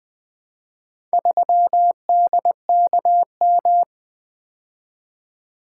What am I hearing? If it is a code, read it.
Morse code, heard as a keyed tone, about 20 wpm, 700 Hz. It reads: 3DKM
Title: Four Letter-Number Combinations - 20wpm